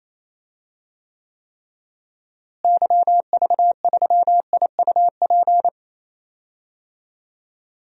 Morse code, heard as a keyed tone, about 28 wpm, 700 Hz. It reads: YV3IUP
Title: DXCC-100 Call Signs - 6 Characters - Repeated 3x - 28wpm